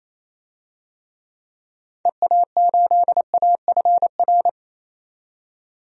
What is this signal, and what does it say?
Morse code, heard as a keyed tone, about 28 wpm, 700 Hz. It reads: EA8AFR
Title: DXCC-100 Call Signs - 6 Characters - 28wpm